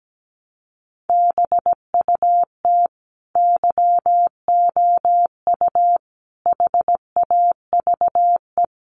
Morse code, heard as a keyed tone, about 17 wpm, 700 Hz. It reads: BUT YOU HAVE
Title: Sentences from Top 1100 Words 17wpm